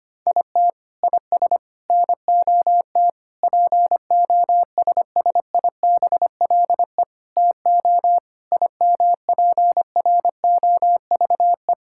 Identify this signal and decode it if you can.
Morse code, heard as a keyed tone, about 25 wpm, 700 Hz. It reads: IT IS NOT POSSIBLE TO IMPROVE
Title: Sentences from Top 1000 Words - Review Part 5 - 25wpm